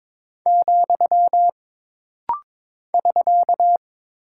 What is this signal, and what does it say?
Morse code, heard as a keyed tone, about 22 wpm, 700 Hz. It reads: , <SK>
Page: This rapid-fire timing is geared to push your ICR (Instant Character Recognition) skills.